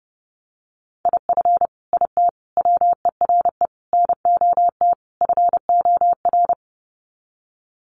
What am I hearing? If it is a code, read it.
Morse code, heard as a keyed tone, about 30 wpm, 700 Hz. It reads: IF IT WERE NOT FOR
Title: Sets of 5 Words 30wpm